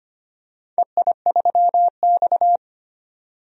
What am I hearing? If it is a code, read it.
Morse code, heard as a keyed tone, about 25 wpm, 700 Hz. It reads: EI3X
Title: DXCC-100 Call Signs - 4 Characters - 25wpm